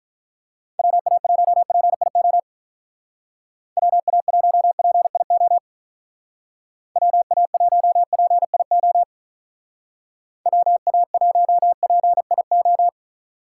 Morse code, heard as a keyed tone, about 43 wpm, 700 Hz. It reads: WA1PIO WA1PIO WA1PIO WA1PIO
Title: Speed Racing - USA 2x3 Call Signs 35wpm